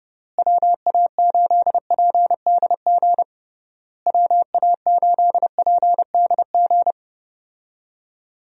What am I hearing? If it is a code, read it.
Morse code, heard as a keyed tone, about 30 wpm, 700 Hz. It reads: WA8PDG WA8PDG
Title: USA Call Signs - 6 Characters - Repeated 3x - 30wpm